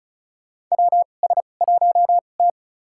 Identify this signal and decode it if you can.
Morse code, heard as a keyed tone, about 36 wpm, spaced as for 25 wpm, 700 Hz. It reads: WS1T
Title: USA Call Signs - 4 Characters - 2x Character Spacing - 35wpm